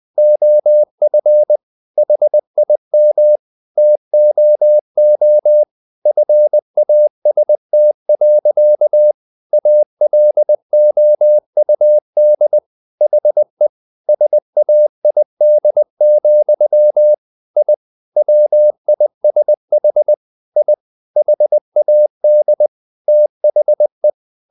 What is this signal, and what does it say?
Morse code, heard as a keyed tone, about 20 wpm, 600 Hz. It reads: OF HIM TOO FAST. ALOUD HE SAID, I WISH I HAD THE
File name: the old man and the sea / the old man and the sea part 1 20wpm